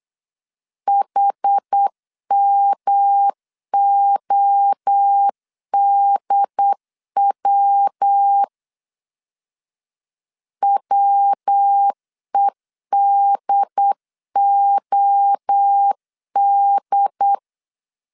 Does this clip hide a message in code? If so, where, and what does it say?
Yes: Morse code from 0.9 to 17.4 s, heard as a keyed tone, about 8 wpm, 800 Hz. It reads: HMODW WEDOD